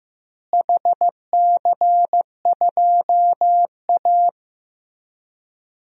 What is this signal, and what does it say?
Morse code, heard as a keyed tone, about 16 wpm, 700 Hz. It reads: HC2A
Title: DXCC-100 Call Signs - 4 Characters - Repeated 3x - 15wpm